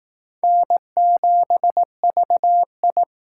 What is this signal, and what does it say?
Morse code, heard as a keyed tone, about 19 wpm, 700 Hz. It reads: N7VI